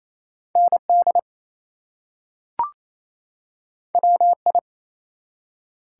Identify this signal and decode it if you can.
Morse code, heard as a keyed tone, about 28 wpm, 700 Hz. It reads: ND WI
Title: US States and Canadian Provinces 28wpm